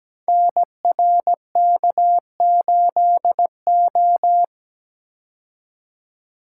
Morse code, heard as a keyed tone, about 17 wpm, 700 Hz. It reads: NRK8O
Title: Code Groups - 5 Letter:Numbers - 17wpm